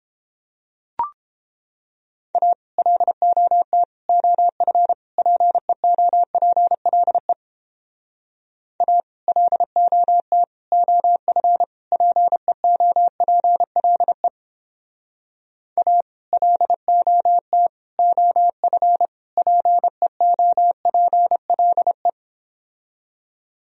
Morse code, heard as a keyed tone, about 31 wpm, 700 Hz. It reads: A LOT OF PEOPLE A LOT OF PEOPLE A LOT OF PEOPLE